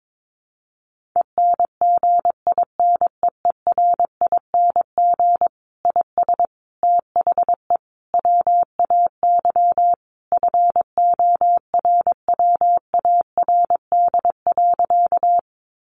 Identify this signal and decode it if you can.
Morse code, heard as a keyed tone, about 22 wpm, 700 Hz. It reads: ENGINEERING IS THE WAY FORWARD.
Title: Sentences from Top 1500 Words 22wpm